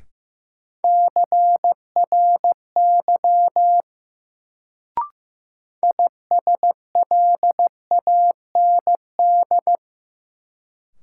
Morse code, heard as a keyed tone, about 15 wpm, 700 Hz. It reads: CRY ISLAND
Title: Sentences from Top 500 Words - New Word Drill - 15wpm